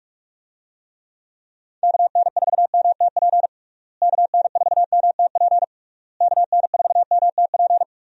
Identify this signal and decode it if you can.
Morse code, heard as a keyed tone, about 46 wpm, 700 Hz. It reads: KN4MTP KN4MTP KN4MTP
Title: USA Call Signs - 6 Characters - Repeated 3x - 45wpm